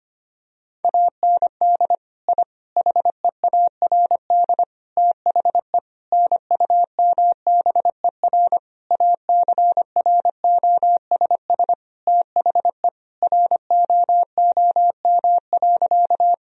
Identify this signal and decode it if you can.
Morse code, heard as a keyed tone, about 25 wpm, 700 Hz. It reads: AND I HEARD THE NUMBER ACROSS THE ROOM.
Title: Sentences from Top 1000 Words - Review Part 1 - 25wpm